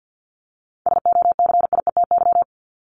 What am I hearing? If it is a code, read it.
Morse code, heard as a keyed tone, about 50 wpm, 700 Hz. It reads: SOCIETY
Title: Top 501-750 Words 50wpm